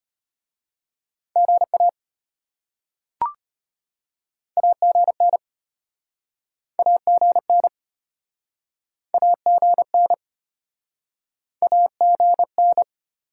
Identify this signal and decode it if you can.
Morse code, heard as a keyed tone, about 32 wpm, 700 Hz. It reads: GA AGN AGN AGN AGN